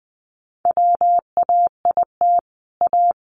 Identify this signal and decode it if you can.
Morse code, heard as a keyed tone, about 20 wpm, 700 Hz. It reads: WAIT A
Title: Sentences from Top 500 Words 20wpm